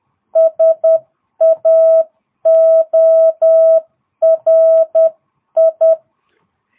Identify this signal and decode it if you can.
Morse code, heard as a keyed tone, about 10 wpm, 650 Hz. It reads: SAORI